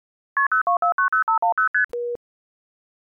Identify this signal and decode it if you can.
Multi-frequency trunk signalling, heard as fast pairs of tones, tones of about 120 ms. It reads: KP02490510ST
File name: Clean Phone Tones 24371